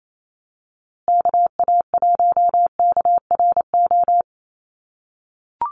Morse code, heard as a keyed tone, about 28 wpm, 700 Hz. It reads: KA1KRO
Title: USA Call Signs Repeated 2x 28wpm